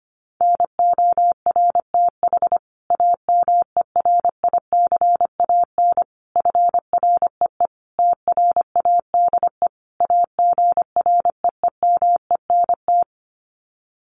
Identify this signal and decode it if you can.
Morse code, heard as a keyed tone, about 25 wpm, 700 Hz. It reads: NORTH AMERICAN FREE TRADE AGREEMENT
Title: Sets of 5 Words Encore 25wpm